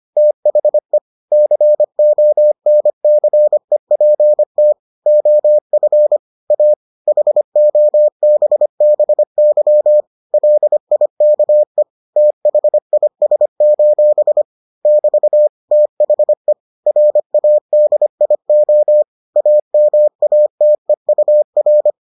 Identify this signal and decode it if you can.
Morse code, heard as a keyed tone, about 25 wpm, 600 Hz. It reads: THE CONCEPT OF A HOBBY LIKE THIS: <BT> THE RADIO AMATEUR